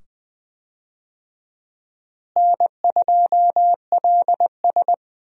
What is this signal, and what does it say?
Morse code, heard as a keyed tone, about 20 wpm, 700 Hz. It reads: N2LS
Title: USA Call Signs - 4 Characters - Repeated 2x - 20wpm